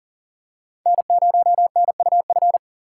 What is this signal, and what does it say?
Morse code, heard as a keyed tone, about 40 wpm, 700 Hz. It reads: N0NUF
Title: USA Call Signs - 5 Characters - 6x Character Spacing - 40wpm